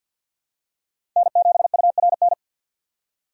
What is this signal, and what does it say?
Morse code, heard as a keyed tone, about 52 wpm, 700 Hz. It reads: N7URN
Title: USA Call Signs - 5 Characters - 50wpm